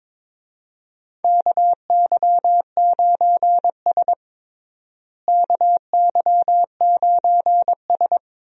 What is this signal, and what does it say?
Morse code, heard as a keyed tone, about 22 wpm, 700 Hz. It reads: KY9S KY9S
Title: USA Call Signs - 4 Characters - Repeated 2x - 22wpm